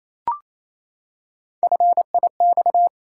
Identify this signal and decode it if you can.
Morse code, heard as a keyed tone, about 28 wpm, 700 Hz. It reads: FIX
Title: Three Letter Words - 28wpm